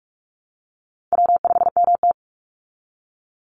Morse code, heard as a keyed tone, about 44 wpm, 700 Hz. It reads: W5MT
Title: Easy Call Signs Repeated 2x - 45wpm